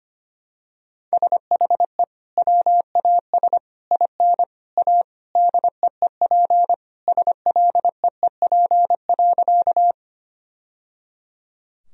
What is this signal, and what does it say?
Morse code, heard as a keyed tone, about 25 wpm, 700 Hz. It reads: SHE WAS IN A DEEP SLEEP.
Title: Sentences from Top 1000 Words - Review Part 3 - 25wpm